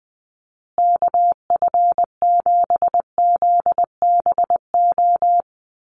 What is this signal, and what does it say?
Morse code, heard as a keyed tone, about 20 wpm, 700 Hz. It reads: KF7ZBO
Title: USA Call Signs Repeated 3x 20wpm